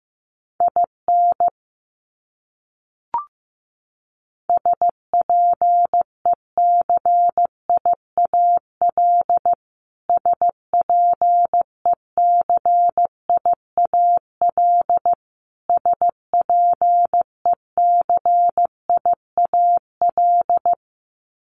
Morse code, heard as a keyed tone, about 15 wpm, 700 Hz. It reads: IN SPECIAL SPECIAL SPECIAL
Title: Top 500 Words Repeated 15wpm